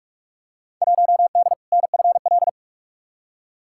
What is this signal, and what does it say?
Morse code, heard as a keyed tone, about 43 wpm, 700 Hz. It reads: 1D NFL
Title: Field Day Exchanges - 45wpm